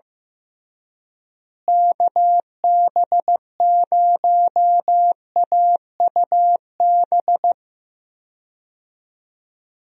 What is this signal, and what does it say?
Morse code, heard as a keyed tone, about 16 wpm, 700 Hz. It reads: KB0AUB
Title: USA Call Signs - 6 Characters - Repeated 3x - 15wpm